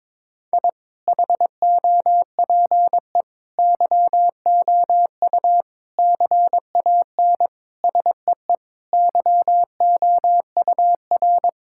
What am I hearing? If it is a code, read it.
Morse code, heard as a keyed tone, about 22 wpm, 700 Hz. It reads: I HOPE YOU CAN SEE YOUR
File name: practice / 22wpm / Sets of 8 Words - 22wpm